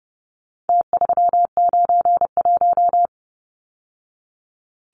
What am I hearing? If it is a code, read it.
Morse code, heard as a keyed tone, about 30 wpm, 700 Hz. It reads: T391
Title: Numbers - Contest Style - 4-Digit 30wpm